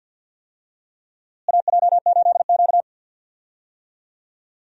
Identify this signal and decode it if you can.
Morse code, heard as a keyed tone, about 53 wpm, 700 Hz. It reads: AJ8Q
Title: USA Call Signs - 4 Characters - Repeated 3x - 50wpm